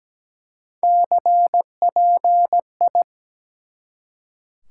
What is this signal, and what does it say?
Morse code, heard as a keyed tone, about 17 wpm, 700 Hz. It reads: CPI
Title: Top 90 QSO Elements 17wpm